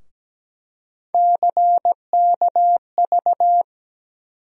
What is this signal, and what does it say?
Morse code, heard as a keyed tone, about 17 wpm, 700 Hz. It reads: CKV
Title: USA Call Sign Postfixes - 3 Letters 17wpm